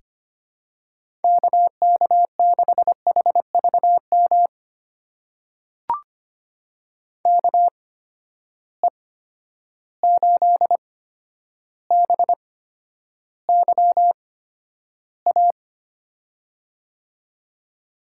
Morse code, heard as a keyed tone, about 25 wpm, spaced as for 11 wpm, 700 Hz. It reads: KK6HVM KE8BYA
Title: USA Call Signs - 6 Characters - 8x Character Spacing - 25wpm